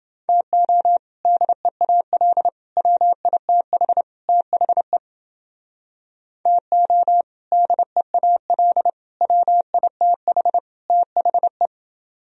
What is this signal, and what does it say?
Morse code, heard as a keyed tone, about 29 wpm, 700 Hz. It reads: TO DEAL WITH THE TO DEAL WITH THE